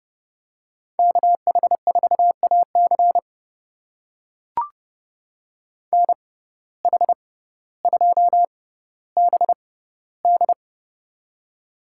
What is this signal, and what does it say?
Morse code, heard as a keyed tone, about 31 wpm, spaced as for 16 wpm, 700 Hz. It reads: KH4AC NH2BD
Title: USA Call Signs - 5 Characters - 6x Character Spacing - 30wpm